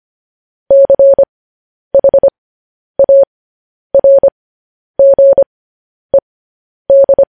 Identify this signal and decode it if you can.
Morse code, heard as a keyed tone, about 25 wpm, spaced as for 10 wpm, 550 Hz. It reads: CHARGED
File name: podcast / News headlines-1x-25x10